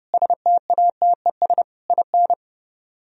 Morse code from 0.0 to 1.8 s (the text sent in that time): STATES